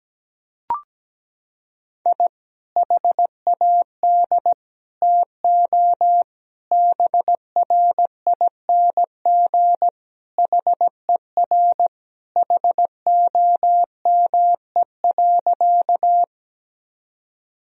Morse code, heard as a keyed tone, about 17 wpm, 700 Hz. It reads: I HAD TO BRING HER HOME.
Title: Sentences from Top 1000 Words - Review Part 3 - 17wpm